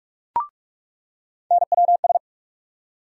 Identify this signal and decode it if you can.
Morse code, heard as a keyed tone, about 47 wpm, 700 Hz. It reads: NWS
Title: USA Call Sign Postfixes - 3 Letters 45wpm